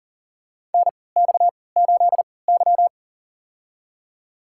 Morse code, heard as a keyed tone, about 40 wpm, spaced as for 23 wpm, 700 Hz. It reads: NX8Y
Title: USA Call Signs - 4 Characters - 3x Character Spacing - 40wpm